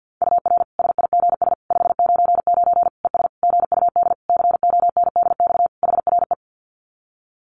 Morse code, heard as a keyed tone, about 50 wpm, 700 Hz. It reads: UR SIGS 599 ES GUD CONDX HRE
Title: Common QSO Phrases 50wpm